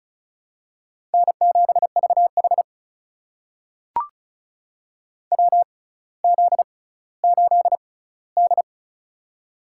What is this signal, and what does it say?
Morse code, heard as a keyed tone, about 36 wpm, spaced as for 18 wpm, 700 Hz. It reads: N7VH WZ8D
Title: USA Call Signs - 4 Characters - 6x Character Spacing - 35wpm